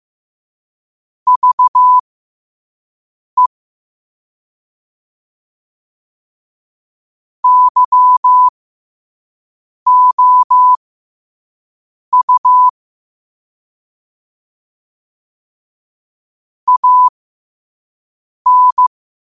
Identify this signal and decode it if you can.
Morse code, heard as a keyed tone, about 15 wpm, spaced as for 5 wpm, 1000 Hz. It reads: VE YOU AN
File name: qotdfiles / 5wpm